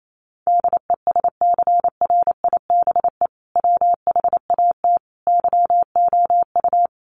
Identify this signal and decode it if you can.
Morse code, heard as a keyed tone, about 28 wpm, 700 Hz. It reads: DESCRIBE WHAT YOU
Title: Sentences from Top 1000 Words 28wpm